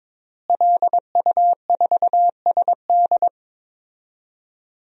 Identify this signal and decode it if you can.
Morse code, heard as a keyed tone, about 22 wpm, 700 Hz. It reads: LU4SD
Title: DXCC-100 Call Signs - 5 Characters - Repeated 2x - 22wpm